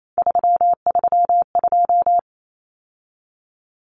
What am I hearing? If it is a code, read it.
Morse code, heard as a keyed tone, about 28 wpm, 700 Hz. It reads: 332
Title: Common Cut Numbers - 3-Digit 28wpm